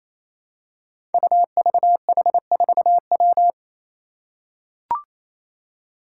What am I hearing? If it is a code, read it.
Morse code, heard as a keyed tone, about 28 wpm, 700 Hz. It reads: UVH4W
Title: Five Challenging Letter-Number Combinations 28wpm